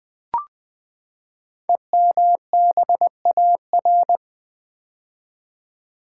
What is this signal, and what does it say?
Morse code, heard as a keyed tone, about 20 wpm, 700 Hz. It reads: EMBAR
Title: Build A Word - 5 Letters At A Time - 12-Letter Words - 20wpm